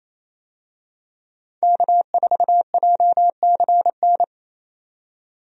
Morse code, heard as a keyed tone, about 28 wpm, 700 Hz. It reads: K4JCN
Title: USA Call Signs - 5 Characters - 28wpm